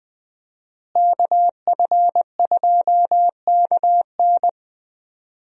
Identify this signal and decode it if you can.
Morse code, heard as a keyed tone, about 20 wpm, 700 Hz. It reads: KF2KN
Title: USA Call Signs - 5 Characters - Repeated 2x - 20wpm